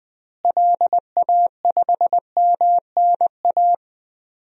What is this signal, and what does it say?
Morse code, heard as a keyed tone, about 20 wpm, 700 Hz. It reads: LA5MNA
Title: DXCC-100 Call Signs - 6 Characters - Repeated 2x - 20wpm